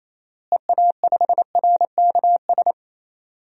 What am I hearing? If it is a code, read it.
Morse code, heard as a keyed tone, about 27 wpm, 700 Hz. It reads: EA5RKS
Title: DXCC-100 Call Signs - 6 Characters - Repeated 2x - 28wpm